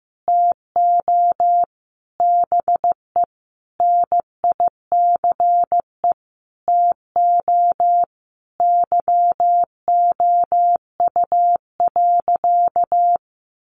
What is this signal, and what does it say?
Morse code, heard as a keyed tone, about 15 wpm, 700 Hz. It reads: TO BE NICE TO YOU.
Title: Sentences from Top 1100 Words 15wpm